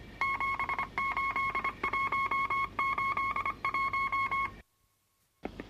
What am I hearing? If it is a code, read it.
Morse code, heard as a keyed tone, about 24 wpm, 1090 Hz, with noise. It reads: 78181
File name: wrct-log hi- 22